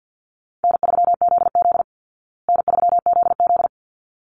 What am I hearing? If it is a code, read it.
Morse code, heard as a keyed tone, about 48 wpm, 700 Hz. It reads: N3ZZ N3ZZ
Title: Easy Call Signs Repeated 2x - 50wpm